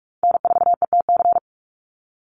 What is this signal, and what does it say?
Morse code, heard as a keyed tone, about 46 wpm, 700 Hz. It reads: N4ETC
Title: Call Signs - 45wpm